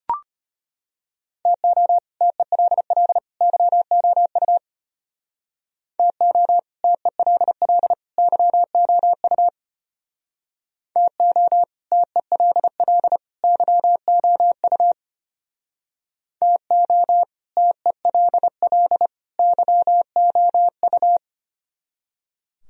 Morse code, heard as a keyed tone, about 33 wpm, 700 Hz. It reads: TO TELL YOU TO TELL YOU TO TELL YOU TO TELL YOU